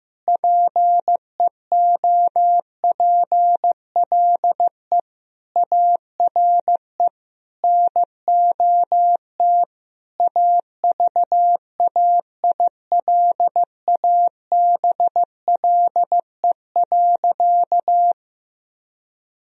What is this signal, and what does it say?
Morse code, heard as a keyed tone, about 15 wpm, 700 Hz. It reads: PEOPLE ARE NOT AVAILABLE.
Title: Sentences from Top 1000 Words - Review Part 3 - 15wpm